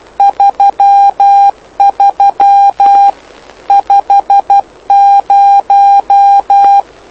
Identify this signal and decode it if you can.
Morse code, heard as a keyed tone, about 12 wpm, 800 Hz. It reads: 33 50
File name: lu4aao / lu7aa 09dic2011 exp planeador vuelo1